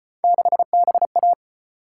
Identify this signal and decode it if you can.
Morse code, heard as a keyed tone, about 31 wpm, 700 Hz. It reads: 6BA
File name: Speed-Racing - Three Letter-Number Combinations - 25wpm